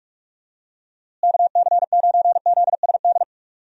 Morse code, heard as a keyed tone, about 45 wpm, 700 Hz. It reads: KC9ZSD
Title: USA Call Signs - 6 Characters - Repeated 3x - 45wpm